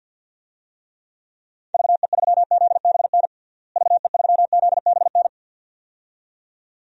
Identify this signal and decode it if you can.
Morse code, heard as a keyed tone, about 53 wpm, 700 Hz. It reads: VE3ZBN VE3ZBN
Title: DXCC-100 Call Signs - 6 Characters - Repeated 2x - 50wpm